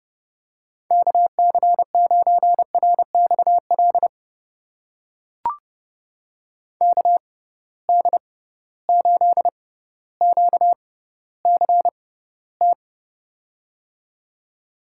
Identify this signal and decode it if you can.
Morse code, heard as a keyed tone, about 30 wpm, spaced as for 15 wpm, 700 Hz. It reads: KC9RXL KD8QCT
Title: USA Call Signs - 6 Characters - 6x Character Spacing - 30wpm